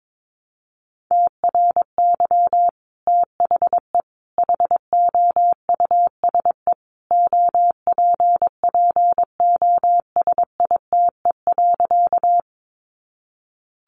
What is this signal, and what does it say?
Morse code, heard as a keyed tone, about 22 wpm, 700 Hz. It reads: TRY THE HOUSE OPPOSITE.
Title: Sentences from Top 1300 Words 22wpm